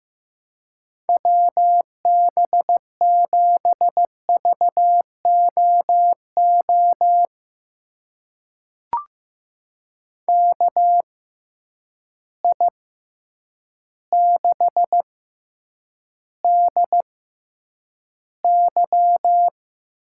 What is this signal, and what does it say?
Morse code, heard as a keyed tone, about 15 wpm, spaced as for 8 wpm, 700 Hz. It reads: WB7VOO KI6DY
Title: USA Call Signs - 6 Characters - 6x Character Spacing - 15wpm